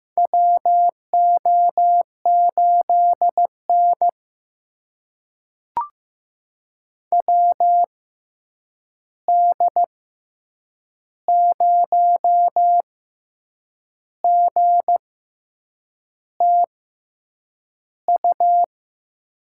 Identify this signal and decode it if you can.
Morse code, heard as a keyed tone, about 15 wpm, spaced as for 7 wpm, 700 Hz. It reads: WO8N WD0GTU
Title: USA Call Signs - 6x Character Spacing - 15wpm